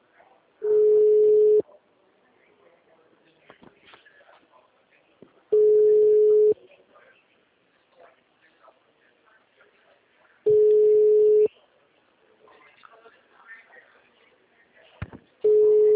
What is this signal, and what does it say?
Ringback tone (Europe, 425 Hz), with speech.
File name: out- -2002- - - .299